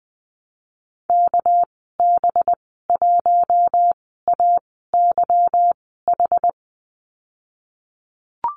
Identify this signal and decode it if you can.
Morse code, heard as a keyed tone, about 20 wpm, spaced as for 15 wpm, 700 Hz. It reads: KB1AYH
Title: 219 Lesson 35g - Number 0 - Call Signs 20wpm